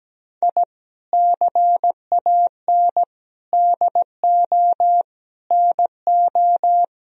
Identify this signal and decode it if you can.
Morse code, heard as a keyed tone, about 17 wpm, 700 Hz. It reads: I CAN DO NO
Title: Sentences from Top 1000 Words - Review Part 2 - 17wpm